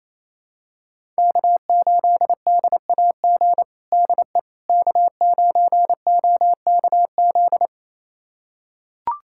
Morse code, heard as a keyed tone, about 28 wpm, 700 Hz. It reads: K8DAG DE K9OKZ
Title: USA Callsign DE Callsign - 5 Characters - 28wpm